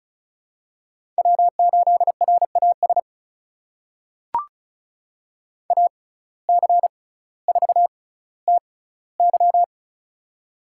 Morse code, heard as a keyed tone, about 36 wpm, spaced as for 18 wpm, 700 Hz. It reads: W8RAS AC4TY
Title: USA Call Signs - 5 Characters - 6x Character Spacing - 35wpm